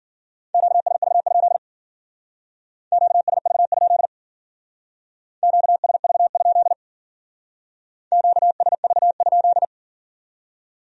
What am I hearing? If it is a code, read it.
Morse code, heard as a keyed tone, about 49 wpm, 700 Hz. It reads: QSV? QSV? QSV? QSV?
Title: Speed-Racing Q-Codes - 40wpm